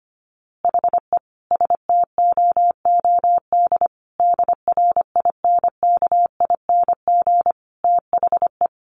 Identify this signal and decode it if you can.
Morse code, heard as a keyed tone, about 25 wpm, 700 Hz. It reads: HE STOOD DRINKING THE
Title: A Sound Of Thunder 25wpm